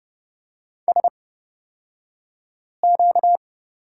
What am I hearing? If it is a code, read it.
Morse code, heard as a keyed tone, about 30 wpm, spaced as for 5 wpm, 700 Hz. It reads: SQ
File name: Lesson 38d - Letter q - Two Characters 30wpm